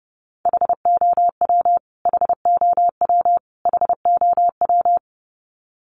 Morse code, heard as a keyed tone, about 30 wpm, 700 Hz. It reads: HOW HOW HOW
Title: Top 300 Words Repeated 30wpm